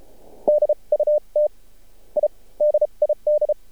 Morse code, heard as a keyed tone, about 30 wpm, spaced as for 26 wpm, 600 Hz. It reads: DUT I DID
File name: archived / CRSnet